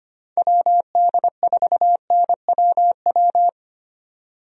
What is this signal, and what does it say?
Morse code, heard as a keyed tone, about 25 wpm, 700 Hz. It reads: WD4NWW
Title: USA Call Signs - 6 Characters - 25wpm